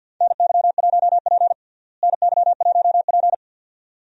Morse code, heard as a keyed tone, about 48 wpm, 700 Hz. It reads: NY1P NY1P
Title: USA Call Signs - 4 Characters - Repeated 2x - 50wpm